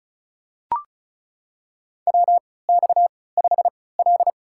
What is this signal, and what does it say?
Morse code, heard as a keyed tone, about 36 wpm, spaced as for 20 wpm, 700 Hz. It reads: WX5L
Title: USA Call Signs - 3x Character Spacing - 35wpm